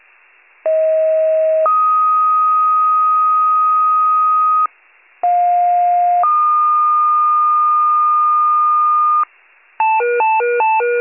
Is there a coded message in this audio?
Yes: two-tone page from 0.6 to 9.2 s, heard as a one-second tone, then a three-second tone, about 640 then 1250 Hz, about 710 then 1220 Hz.